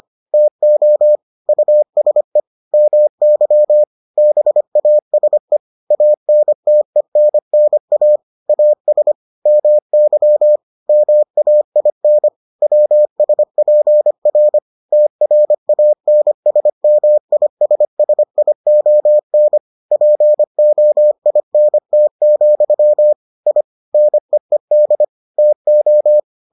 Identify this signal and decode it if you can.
Morse code, heard as a keyed tone, about 25 wpm, 600 Hz. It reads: TO USE MY BASE ANTENNA AS MY MAIN WSPR TRANSMISSION POINT, I NEED TO